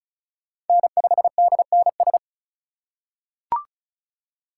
Morse code, heard as a keyed tone, about 36 wpm, 700 Hz. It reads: N5DNS
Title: USA Call Signs - 5 Characters - 3x Character Spacing - 35wpm